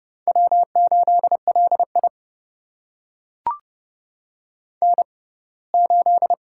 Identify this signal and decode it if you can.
Morse code, heard as a keyed tone, about 30 wpm, spaced as for 20 wpm, 700 Hz. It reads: W8LI N8
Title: USA Call Signs - 4 Characters - 6x Character Spacing - 30wpm